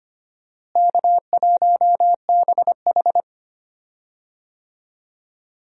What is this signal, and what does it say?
Morse code, heard as a keyed tone, about 25 wpm, 700 Hz. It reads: K1BH
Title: USA Call Signs - 4 Characters - Repeated 2x - 25wpm